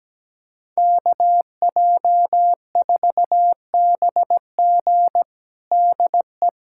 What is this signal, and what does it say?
Morse code, heard as a keyed tone, about 17 wpm, 700 Hz. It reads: KJ4BG DE
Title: USA Callsign DE Callsign 17wpm